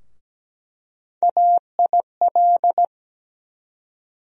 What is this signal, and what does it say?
Morse code, heard as a keyed tone, about 17 wpm, 700 Hz. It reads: AIL
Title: USA Call Sign Postfixes - 3 Letters 17wpm